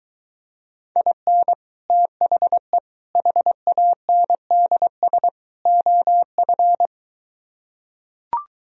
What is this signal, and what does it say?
Morse code, heard as a keyed tone, about 23 wpm, 700 Hz. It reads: IN THE HANDS OF